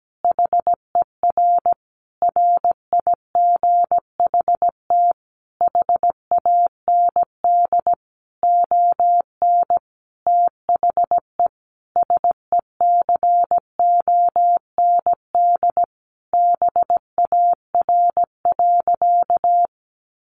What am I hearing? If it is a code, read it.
Morse code, heard as a keyed tone, about 17 wpm, 700 Hz. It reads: HER RIGHT HAND ON THE SECOND BAR.
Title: Easy Phrases 17wpm